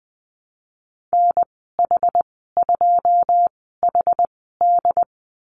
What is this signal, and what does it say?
Morse code, heard as a keyed tone, about 20 wpm, 700 Hz. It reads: N H 2 H D
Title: 082 Lesson 14g - Number 2 - Call Signs 20wpm